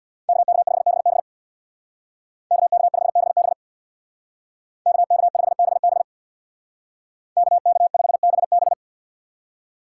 Morse code, heard as a keyed tone, about 64 wpm, 700 Hz. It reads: KK5BB KK5BB KK5BB KK5BB